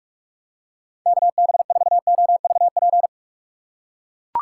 Morse code, heard as a keyed tone, about 44 wpm, 700 Hz. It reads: KB4OVP
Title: USA Call Signs - 6 Characters - 6x Character Spacing - 45wpm